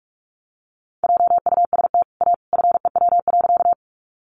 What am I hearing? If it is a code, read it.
Morse code, heard as a keyed tone, about 44 wpm, 700 Hz. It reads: JUST A FEW.
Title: Sentences from Top 300 Words 45wpm